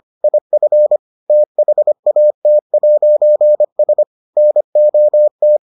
Morse code, heard as a keyed tone, about 25 wpm, 600 Hz. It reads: IF THAT'S NOT